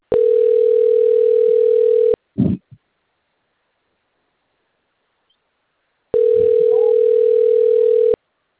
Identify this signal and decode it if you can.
Ringback tone (North America), 440 + 480 Hz.